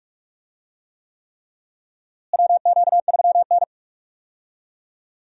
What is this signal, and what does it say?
Morse code, heard as a keyed tone, about 46 wpm, 700 Hz. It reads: WQ3N
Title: USA Call Signs - 4 Characters - Repeated 3x - 45wpm